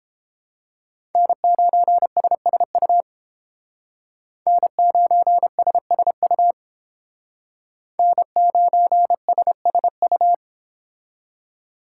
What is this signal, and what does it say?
Morse code, heard as a keyed tone, about 31 wpm, 700 Hz. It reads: N9SSU N9SSU N9SSU